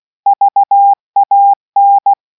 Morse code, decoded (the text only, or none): VAN